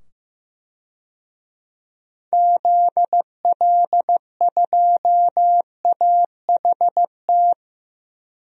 Morse code, heard as a keyed tone, about 15 wpm, 700 Hz. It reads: ZL2AHT
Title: DXCC-100 Call Signs - 6 Characters - 15wpm